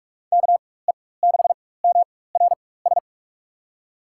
Morse code, heard as a keyed tone, about 46 wpm, spaced as for 21 wpm, 700 Hz. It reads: KE6MRS
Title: USA Call Signs - 4x Character Spacing - 45wpm